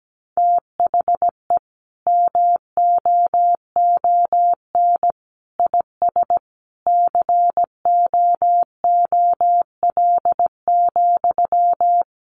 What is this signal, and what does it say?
Morse code, heard as a keyed tone, about 17 wpm, 700 Hz. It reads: THE MOON IS COOL,
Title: Sentences from Top 1200 Words 17wpm